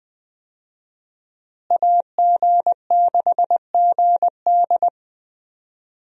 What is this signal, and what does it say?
Morse code, heard as a keyed tone, about 20 wpm, 700 Hz. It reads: AG6GD
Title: USA Call Signs - 5 Characters - Repeated 2x - 20wpm